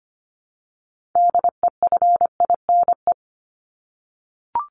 Morse code, heard as a keyed tone, about 25 wpm, 700 Hz. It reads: DEFINE
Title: ED Suffix - Top 100 Words with Base Words 25wpm